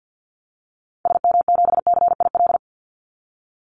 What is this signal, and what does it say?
Morse code, heard as a keyed tone, about 50 wpm, 700 Hz. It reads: SM7CIL
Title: Call Signs Repeated 2x - 50wpm